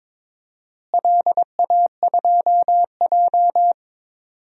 Morse code, heard as a keyed tone, about 22 wpm, 700 Hz. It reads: LA2J
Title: DXCC-100 Call Signs - 4 Characters - Repeated 2x - 22wpm